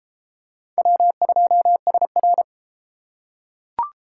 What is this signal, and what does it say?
Morse code, heard as a keyed tone, about 29 wpm, 700 Hz. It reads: W2SR